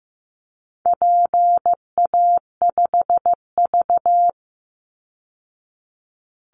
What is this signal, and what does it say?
Morse code, heard as a keyed tone, about 15 wpm, 700 Hz. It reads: PA5V
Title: Call Signs Repeated 3x - 15wpm